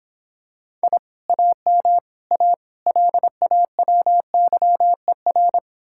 Morse code, heard as a keyed tone, about 26 wpm, 700 Hz. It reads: I AM A LAWYER